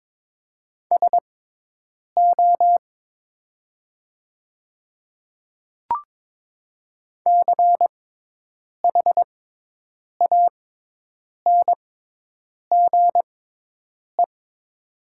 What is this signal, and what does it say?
Morse code, heard as a keyed tone, about 22 wpm, spaced as for 8 wpm, 700 Hz. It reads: SO CHANGE
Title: 6x Character Spacing - Top 200 Words - 22wpm